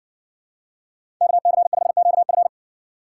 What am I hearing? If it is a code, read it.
Morse code, heard as a keyed tone, about 60 wpm, 700 Hz. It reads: KC5QF